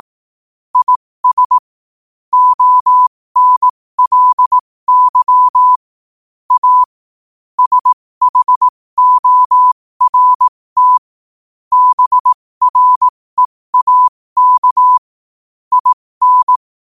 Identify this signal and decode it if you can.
Morse code, heard as a keyed tone, about 18 wpm, spaced as for 15 wpm, 1000 Hz. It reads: IS ONLY A SHORT BREAK IN
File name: qotdfiles / 15wpm